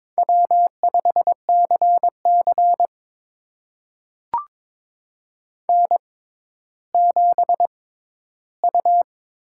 Morse code, heard as a keyed tone, about 22 wpm, spaced as for 12 wpm, 700 Hz. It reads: W5CC N7U
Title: USA Call Signs - 4 Characters - 6x Character Spacing - 22wpm